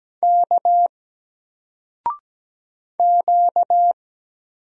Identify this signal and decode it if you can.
Morse code, heard as a keyed tone, about 17 wpm, 700 Hz. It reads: K Q
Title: Single Letter-Number 17wpm